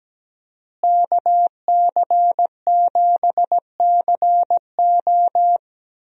Morse code, heard as a keyed tone, about 17 wpm, 700 Hz. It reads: KC7CO
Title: USA Call Signs - 5 Characters - Repeated 2x - 17wpm